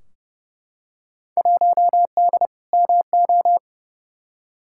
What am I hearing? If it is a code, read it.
Morse code, heard as a keyed tone, about 30 wpm, 700 Hz. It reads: 1D MO
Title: Field Day Exchanges - 30wpm